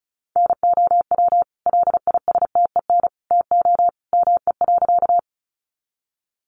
Morse code, heard as a keyed tone, about 34 wpm, 700 Hz. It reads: NOW LISTEN TO ME.
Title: Sentences from Top 400 Words 35wpm